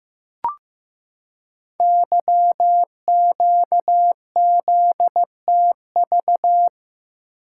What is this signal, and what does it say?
Morse code, heard as a keyed tone, about 15 wpm, 700 Hz. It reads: YQZTV
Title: Code Groups - 5 Letter:Numbers - 15wpm